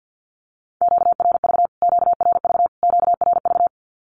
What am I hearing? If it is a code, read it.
Morse code, heard as a keyed tone, about 50 wpm, 700 Hz. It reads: QRV QRV QRV
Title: Q-Codes Repeated 3x 50wpm